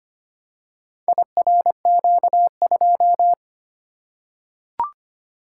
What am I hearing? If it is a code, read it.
Morse code, heard as a keyed tone, about 25 wpm, 700 Hz. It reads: IRQ2
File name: practice / 25wpm / Four Letter-Number Combinations - 25wpm